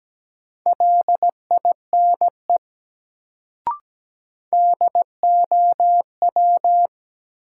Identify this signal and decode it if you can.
Morse code, heard as a keyed tone, about 17 wpm, 700 Hz. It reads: LINE DOW
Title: Sets of 5 Words - New Word Drill - 17wpm